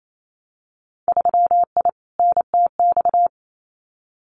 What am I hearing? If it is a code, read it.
Morse code, heard as a keyed tone, about 28 wpm, 700 Hz. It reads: 3I NTX
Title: Winter Field Day Exchanges 28wpm